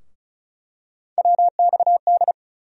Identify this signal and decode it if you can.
Morse code, heard as a keyed tone, about 34 wpm, 700 Hz. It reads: WXD
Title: USA Call Sign Postfixes - 3 Letters 35wpm